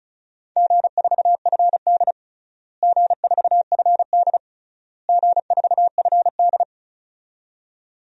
Morse code, heard as a keyed tone, about 34 wpm, 700 Hz. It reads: G4FD G4FD G4FD
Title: DXCC-100 Call Signs - 4 Characters - Repeated 3x - 35wpm